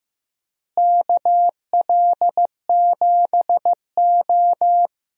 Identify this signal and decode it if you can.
Morse code, heard as a keyed tone, about 15 wpm, 700 Hz. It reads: KL7O
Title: DXCC-100 Call Signs - 4 Characters - Repeated 3x - 15wpm